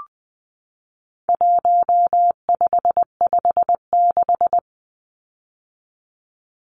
Morse code, heard as a keyed tone, about 20 wpm, 700 Hz. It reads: 1556
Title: Numbers - 4-Digit 20wpm